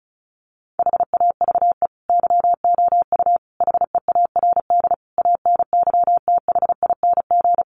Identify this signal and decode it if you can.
Morse code, heard as a keyed tone, about 34 wpm, 700 Hz. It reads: HAVE YOU HEARD ANYTHING
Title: Sentences from Top 1000 Words 35wpm